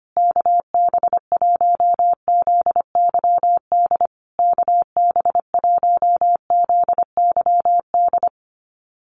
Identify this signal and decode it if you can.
Morse code, heard as a keyed tone, about 25 wpm, 700 Hz. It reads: KB1ZYD KB1ZYD
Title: USA Call Signs Repeated 3x 25wpm